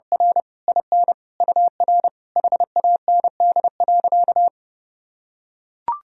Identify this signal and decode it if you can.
Morse code, heard as a keyed tone, about 32 wpm, 700 Hz. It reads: R IN UR HAND.
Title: Sentences from Top 400 Words and Abbreviations 30wpm